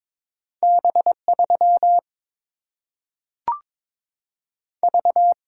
Morse code, heard as a keyed tone, about 22 wpm, 700 Hz. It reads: B 3 V
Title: Single-Letter Single-Number 22wpm